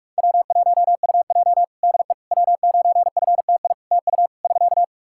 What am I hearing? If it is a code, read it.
Morse code, heard as a keyed tone, about 45 wpm, 700 Hz. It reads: W1UJ DE W9FTI TU <SK>
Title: Practice QSO - Exchange 4 - Ending - 45wpm